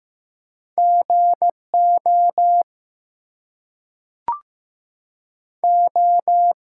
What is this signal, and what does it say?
Morse code, heard as a keyed tone, about 15 wpm, 700 Hz. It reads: GO O
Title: Two Letter Words - 15wpm